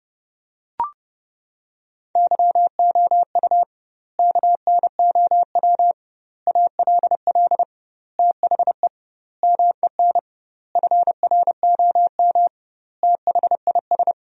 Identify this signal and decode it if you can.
Morse code, heard as a keyed tone, about 30 wpm, 700 Hz. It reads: YOU KNOW ALL THE MEN FROM THIS
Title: 2x Word-Spacing - Sentences from Top 300 Words - 30wpm